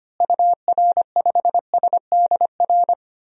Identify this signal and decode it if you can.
Morse code, heard as a keyed tone, about 25 wpm, 700 Hz. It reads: UR5SDR
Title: DXCC-100 Call Signs - 6 Characters - Repeated 3x - 25wpm